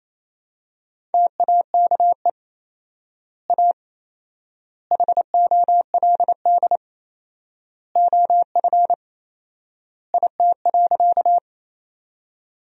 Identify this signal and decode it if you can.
Morse code, heard as a keyed tone, about 28 wpm, 700 Hz. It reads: TAKE A HOLD OF IT.
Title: 4x Word-Spacing - Sentences from Top 400 Words - 28wpm